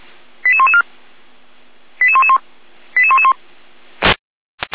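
Five-tone selective call (ZVEI), heard as quick runs of five tones, tones of about 80 ms. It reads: 88114 88111 88111
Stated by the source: Five-tone ZVEI selective call 88112, received by SDR.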